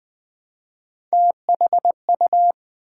Morse code, heard as a keyed tone, about 20 wpm, 700 Hz. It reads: THU
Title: Build A Word - 1 Letter At A Time - 4-Letter Words - 20wpm